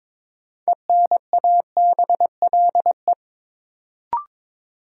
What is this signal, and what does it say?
Morse code, heard as a keyed tone, about 22 wpm, 700 Hz. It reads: ENABLE
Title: Sentences from Top 1500 Words - New Word Drill Alt - 22wpm